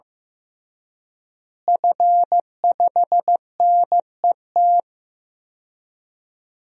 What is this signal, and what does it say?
Morse code, heard as a keyed tone, about 15 wpm, 700 Hz. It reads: F5NET
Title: DXCC-100 Call Signs - 5 Characters - Repeated 3x - 15wpm